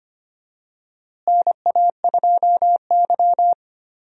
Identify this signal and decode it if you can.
Morse code, heard as a keyed tone, about 25 wpm, 700 Hz. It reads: NA2Y
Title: USA Call Signs - 4 Characters - Repeated 3x - 25wpm